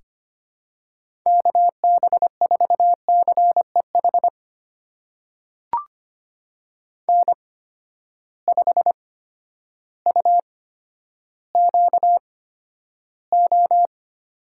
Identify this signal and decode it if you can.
Morse code, heard as a keyed tone, about 25 wpm, spaced as for 11 wpm, 700 Hz. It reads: KB4CEH N5UQO
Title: USA Call Signs - 8x Character Spacing - 25wpm